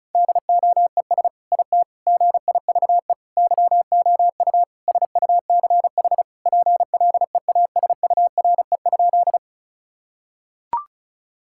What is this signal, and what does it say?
Morse code, heard as a keyed tone, about 36 wpm, 700 Hz. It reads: DOES IT GIVE YOU SUCH PLEASURE?
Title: Sentences from Top 1600 Words 35wpm